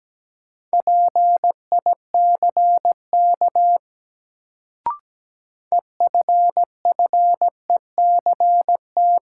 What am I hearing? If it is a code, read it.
Morse code, heard as a keyed tone, about 17 wpm, 700 Hz. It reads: PICK EFFECT
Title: Sentences from Top 1000 Words - New Word Drill - 17wpm